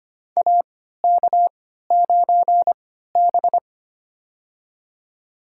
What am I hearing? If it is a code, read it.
Morse code, heard as a keyed tone, about 25 wpm, spaced as for 14 wpm, 700 Hz. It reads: AK9B
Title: USA Call Signs - 4 Characters - 3x Character Spacing - 25wpm